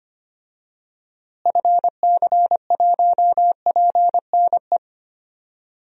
Morse code, heard as a keyed tone, about 25 wpm, 700 Hz. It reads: FC1PNE
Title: DXCC-100 Call Signs - 6 Characters - Repeated 2x - 25wpm